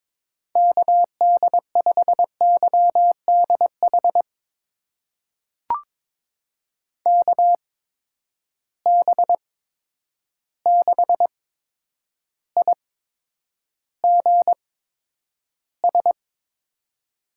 Morse code, heard as a keyed tone, about 22 wpm, spaced as for 9 wpm, 700 Hz. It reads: KD5YDH KB6IGS
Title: USA Call Signs - 6 Characters - 8x Character Spacing - 22wpm